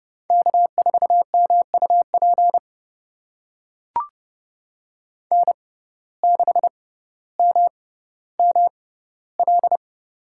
Morse code, heard as a keyed tone, about 32 wpm, spaced as for 16 wpm, 700 Hz. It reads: K4MUP N6MML
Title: USA Call Signs - 5 Characters - 6x Character Spacing - 30wpm